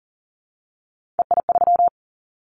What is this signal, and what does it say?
Morse code, heard as a keyed tone, about 40 wpm, 700 Hz. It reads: EI3
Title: Callsign Prefixes for Top 50 DXCC - 40wpm